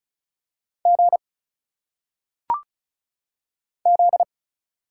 Morse code, heard as a keyed tone, about 35 wpm, 700 Hz. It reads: G Z
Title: Single Letters - Phonetic Pronunciation 35wpm